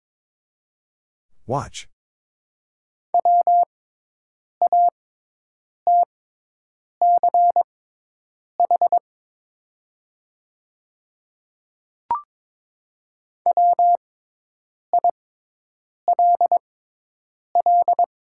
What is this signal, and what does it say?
Morse code, heard as a keyed tone, about 22 wpm, spaced as for 8 wpm, 700 Hz. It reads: E WATCH WILL
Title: 6x Character Spacing - Top 300 Words - 22wpm